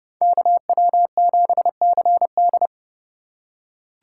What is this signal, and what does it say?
Morse code, heard as a keyed tone, about 29 wpm, 700 Hz. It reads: KW7CD